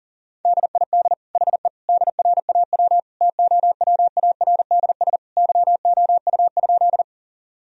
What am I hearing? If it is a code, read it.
Morse code, heard as a keyed tone, about 40 wpm, 700 Hz. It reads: DID HE DRAW TOWARDS YOU?
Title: Sentences from Top 1000 Words - Review Part 2 - 40wpm